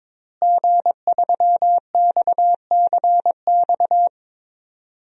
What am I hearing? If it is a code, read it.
Morse code, heard as a keyed tone, about 22 wpm, 700 Hz. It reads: G3XCX
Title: DXCC-100 Call Signs - 5 Characters - Repeated 3x - 22wpm